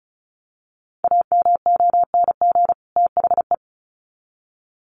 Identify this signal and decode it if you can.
Morse code, heard as a keyed tone, about 34 wpm, 700 Hz. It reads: AMONG THE
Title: Sets of 2 Words 35wpm